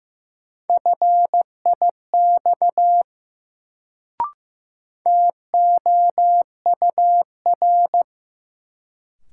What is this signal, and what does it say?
Morse code, heard as a keyed tone, about 15 wpm, 700 Hz. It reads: FIX TOUR
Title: Sentences from Top 1400 Words - New Word Drill Alt - 15wpm